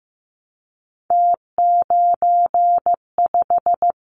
Morse code, heard as a keyed tone, about 15 wpm, 700 Hz. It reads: T95
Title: Callsign Prefixes for Top 100 DXCC - 15wpm